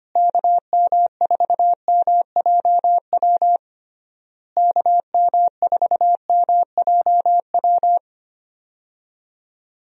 Morse code, heard as a keyed tone, about 25 wpm, 700 Hz. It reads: KM4MJW KM4MJW
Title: USA Call Signs - 6 Characters - Repeated 3x - 25wpm